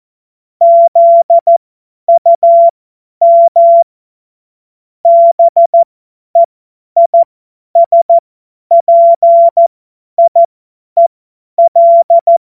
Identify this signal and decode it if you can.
Morse code, heard as a keyed tone, about 14 wpm, spaced as for 10 wpm, 680 Hz. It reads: ZUM BEISPIEL
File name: Morsekurs / Witze / Witz 05. Oktober 2021, 50 F70